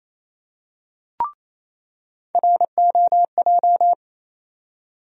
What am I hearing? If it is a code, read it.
Morse code, heard as a keyed tone, about 28 wpm, 700 Hz. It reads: ROJ
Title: USA Call Sign Postfixes - 3 Letters 28wpm